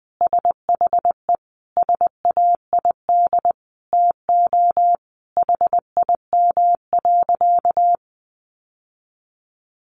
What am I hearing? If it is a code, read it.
Morse code, heard as a keyed tone, about 20 wpm, 700 Hz. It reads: SHE SAID TO HIM.
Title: Sentences from Top 800 Words 20wpm